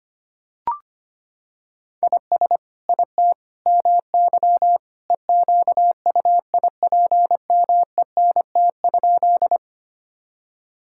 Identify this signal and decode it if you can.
Morse code, heard as a keyed tone, about 25 wpm, 700 Hz. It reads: IS IT MY EQUIPMENT?
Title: Sentences from Top 1000 Words - Review Part 5 - 25wpm